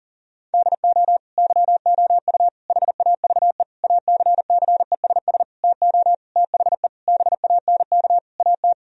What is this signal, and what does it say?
Morse code, heard as a keyed tone, about 40 wpm, 700 Hz. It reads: DO YOU HAVE ACCESS TO THE BANK AT
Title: Sentences from Top 1000 Words - Review Part 4 - 40wpm